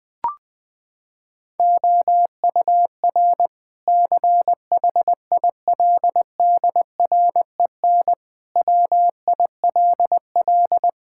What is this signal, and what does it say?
Morse code, heard as a keyed tone, about 20 wpm, 700 Hz. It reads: OUR CHILDREN WILL
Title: Sentences from Top 1000 Words - Review Part 5 - 20wpm